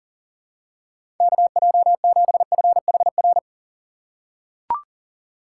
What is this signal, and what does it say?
Morse code, heard as a keyed tone, about 40 wpm, 700 Hz. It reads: KJ7FHR
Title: USA Call Signs - 6 Characters - 2x Character Spacing - 40wpm